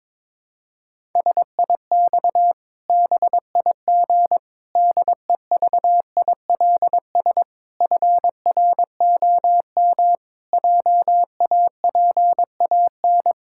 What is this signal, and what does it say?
Morse code, heard as a keyed tone, about 22 wpm, 700 Hz. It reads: SIX BIG DEVILS FROM JAPAN
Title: Pangram Challenge v2 - 22wpm